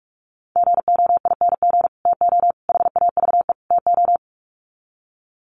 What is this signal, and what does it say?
Morse code, heard as a keyed tone, about 44 wpm, 700 Hz. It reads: GOING TO HAVE TO
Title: Sets of 4 Words 45wpm